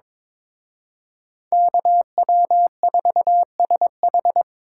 Morse code, heard as a keyed tone, about 26 wpm, 700 Hz. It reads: KW4SH
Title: Speed Racing - USA 2x2 Call Signs 22wpm